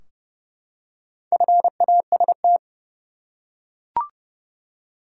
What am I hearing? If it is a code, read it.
Morse code, heard as a keyed tone, about 32 wpm, 700 Hz. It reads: FAST
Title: Four Letter Words - 30wpm